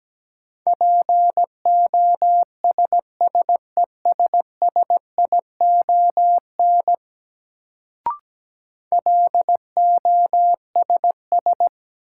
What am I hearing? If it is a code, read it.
Morse code, heard as a keyed tone, about 17 wpm, 700 Hz. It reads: POSSESSION LOSS
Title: Sentences from Top 1400 Words - New Word Drill - 17wpm